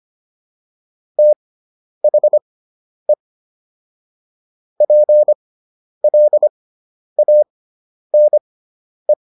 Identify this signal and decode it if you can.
Morse code, heard as a keyed tone, about 25 wpm, spaced as for 10 wpm, 600 Hz. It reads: THE PLANE